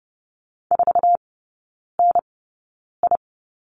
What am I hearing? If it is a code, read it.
Morse code, heard as a keyed tone, about 30 wpm, spaced as for 9 wpm, 700 Hz. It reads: 4NI
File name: 081 Lesson 14e - Number 2 - Three Characters 30wpm